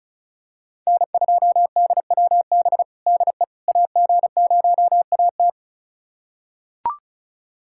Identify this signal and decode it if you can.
Morse code, heard as a keyed tone, about 36 wpm, 700 Hz. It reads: N2DWB DE AG0AT
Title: USA Callsign DE Callsign - 5 Characters - 35wpm